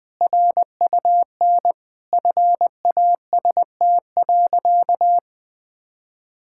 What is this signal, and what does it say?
Morse code, heard as a keyed tone, about 20 wpm, 700 Hz. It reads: RUN FAST.
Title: Sentences with 3-Word Phrases - 20wpm